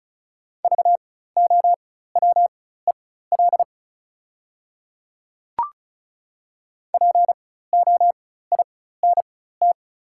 Morse code, heard as a keyed tone, about 36 wpm, spaced as for 16 wpm, 690 Hz. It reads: VOWEL POINT
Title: 4x Character Spacing - Top 500 Words - 35wpm